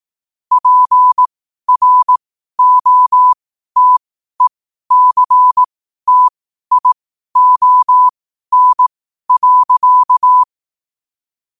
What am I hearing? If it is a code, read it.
Morse code, heard as a keyed tone, about 18 wpm, spaced as for 12 wpm, 1000 Hz. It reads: PROTECTION.
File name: qotdfiles / 12wpm